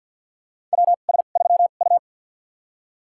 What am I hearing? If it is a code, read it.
Morse code, heard as a keyed tone, about 48 wpm, spaced as for 36 wpm, 700 Hz. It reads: WS3U
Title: USA Call Signs - 4 Characters - 2x Character Spacing - 50wpm